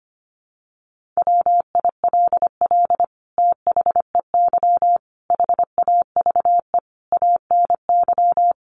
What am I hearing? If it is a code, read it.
Morse code, heard as a keyed tone, about 25 wpm, 700 Hz. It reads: WILL THEY HAVE ANY
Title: Sentences from Top 1500 Words 25wpm